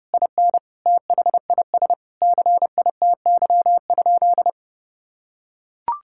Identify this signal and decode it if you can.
Morse code, heard as a keyed tone, about 30 wpm, 700 Hz. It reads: IN THIS CITY?
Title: Sentences from Top 1600 Words 30wpm